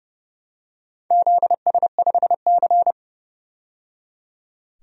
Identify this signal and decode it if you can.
Morse code, heard as a keyed tone, about 30 wpm, 700 Hz. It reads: ZS5C
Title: DXCC-100 Call Signs - 4 Characters - 30wpm